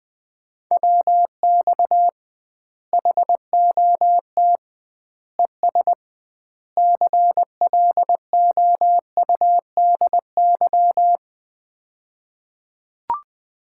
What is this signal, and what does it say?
Morse code, heard as a keyed tone, about 20 wpm, 700 Hz. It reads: WX HOT ES CLOUDY
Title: Common QSO Phrases - 2x Word-Spacing - 20wpm